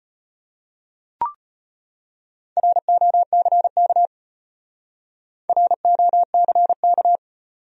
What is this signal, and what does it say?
Morse code, heard as a keyed tone, about 34 wpm, 700 Hz. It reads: ROCK ROCK